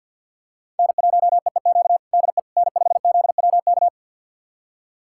Morse code, heard as a keyed tone, about 52 wpm, 700 Hz. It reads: N1EEQ DE N5ZWK
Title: USA Callsign DE Callsign - 5 Characters - 50wpm